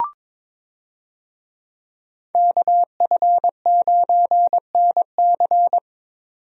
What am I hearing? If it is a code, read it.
Morse code, heard as a keyed tone, about 22 wpm, 700 Hz. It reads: KF9NC
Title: USA Call Signs - 5 Characters - 22wpm